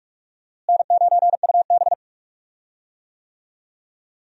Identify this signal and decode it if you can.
Morse code, heard as a keyed tone, about 46 wpm, 700 Hz. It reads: N9UB
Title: USA Call Signs - 4 Characters - Repeated 3x - 45wpm